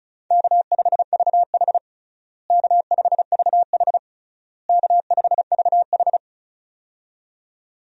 Morse code, heard as a keyed tone, about 36 wpm, 700 Hz. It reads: K5VH K5VH K5VH
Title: USA Call Signs - 4 Characters - Repeated 3x - 35wpm